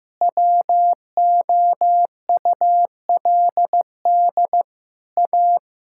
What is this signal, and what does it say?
Morse code, heard as a keyed tone, about 15 wpm, 700 Hz. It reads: WOULD A
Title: Sentences from Top 1000 Words - Review Part 6 - 15wpm